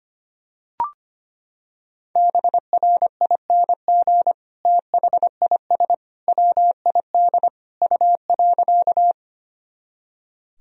Morse code, heard as a keyed tone, about 25 wpm, 700 Hz. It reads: BRING THIS WID U.
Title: Sentences from Top 500 Words and Abbreviations 25wpm